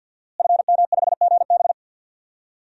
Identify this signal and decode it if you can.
Morse code, heard as a keyed tone, about 52 wpm, 700 Hz. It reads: FM5GB
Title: DXCC-100 Call Signs - 5 Characters - Repeated 3x - 50wpm